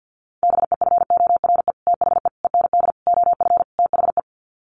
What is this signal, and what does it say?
Morse code, heard as a keyed tone, about 50 wpm, 700 Hz. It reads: BEFORE THE END OF THE
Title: Sets of 5 Words Encore 50wpm